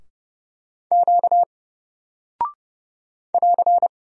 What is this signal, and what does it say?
Morse code, heard as a keyed tone, about 30 wpm, 700 Hz. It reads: Q <AR>
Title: Single Character (Letter:Number:Punctuation:Prosign) Alt 30wpm